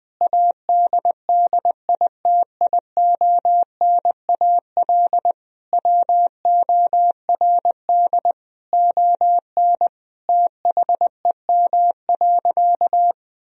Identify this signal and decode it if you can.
Morse code, heard as a keyed tone, about 21 wpm, 700 Hz. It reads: ADDITIONAL WORD ON THEM.
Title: Sentences from Top 1000 Words - Review Part 4 - 20wpm